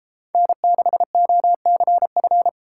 Morse code, heard as a keyed tone, about 33 wpm, 700 Hz. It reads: N6OCF